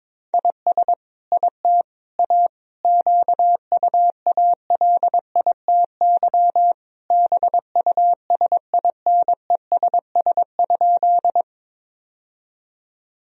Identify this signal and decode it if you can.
Morse code, heard as a keyed tone, about 22 wpm, 700 Hz. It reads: IS IT A QUALITY BUSINESS?
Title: Sentences from Top 1000 Words - Review Part 4 - 22wpm